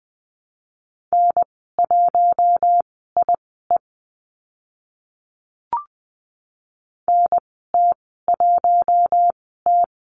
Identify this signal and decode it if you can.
Morse code, heard as a keyed tone, about 20 wpm, spaced as for 14 wpm, 700 Hz. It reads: N1IE NT1T
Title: 042 Lesson 8g - Number 1 - Call Signs 20wpm